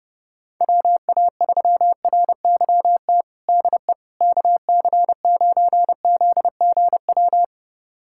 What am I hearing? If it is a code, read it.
Morse code, heard as a keyed tone, about 32 wpm, 700 Hz. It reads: WA3RYT DE KC9ZGW
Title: USA Callsign DE Callsign 30wpm